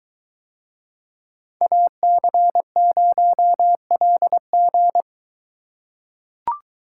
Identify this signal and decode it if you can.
Morse code, heard as a keyed tone, about 21 wpm, 700 Hz. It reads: AC0LG